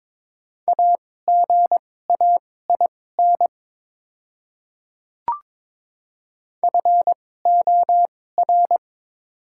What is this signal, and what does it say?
Morse code, heard as a keyed tone, about 22 wpm, spaced as for 16 wpm, 700 Hz. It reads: AGAIN FOR
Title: 2x Character Spacing - Top 300 Words - 22wpm